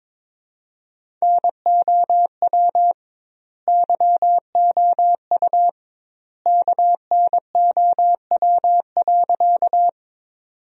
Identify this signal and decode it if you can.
Morse code, heard as a keyed tone, about 22 wpm, 700 Hz. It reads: NOW YOU KNOW.
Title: 2x Word-Spacing - Sentences from Top 100 Words - 22wpm